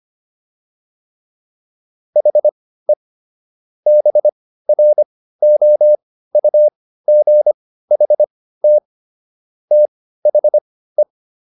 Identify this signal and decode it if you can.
Morse code, heard as a keyed tone, about 25 wpm, spaced as for 15 wpm, 600 Hz. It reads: HE BROUGHT THE